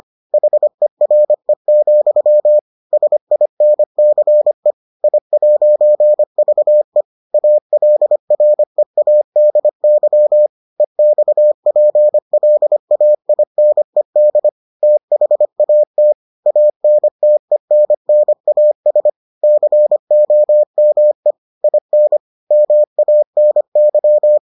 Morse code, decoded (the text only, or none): HERE, SINCE I'VE ALREADY EXPLAINED THAT ANTENNAS COME IN MANY